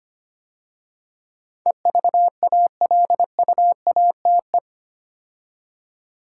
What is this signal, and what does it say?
Morse code, heard as a keyed tone, about 25 wpm, 700 Hz. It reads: EVALUATE
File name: 8 Letter Words - First 3 Letters Given - 25wpm